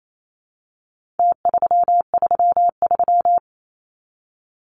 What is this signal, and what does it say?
Morse code, heard as a keyed tone, about 28 wpm, 700 Hz. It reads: T333
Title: Numbers - Contest Style - 4-Digit 28wpm